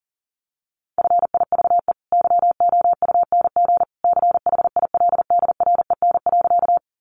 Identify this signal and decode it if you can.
Morse code, heard as a keyed tone, about 37 wpm, 700 Hz. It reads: FIVE YOUNG CHILDREN.
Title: Sentences from Top 400 Words 40wpm